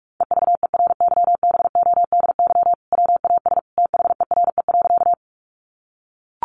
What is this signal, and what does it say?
Morse code, heard as a keyed tone, about 44 wpm, 700 Hz. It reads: EVERYBODY WAS THERE.
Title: Sentences from Top 1400 Words 45wpm